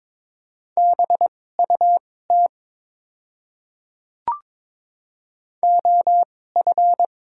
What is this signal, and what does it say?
Morse code, heard as a keyed tone, about 21 wpm, spaced as for 16 wpm, 700 Hz. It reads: BUT OF
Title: 2x Character Spacing - Top 200 Words - 22wpm